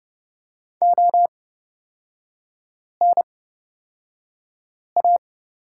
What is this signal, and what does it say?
Morse code, heard as a keyed tone, about 30 wpm, spaced as for 5 wpm, 700 Hz. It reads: ONA
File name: Lesson 37e - Letter x - Three Characters 30wpm